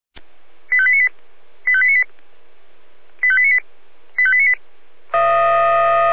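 Five-tone selective call (ZVEI), heard as quick runs of five tones, tones of about 70 ms. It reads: 85898 85898 85898 85898
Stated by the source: Received on an SDR, ZVEI five-tone selective call, channel 496.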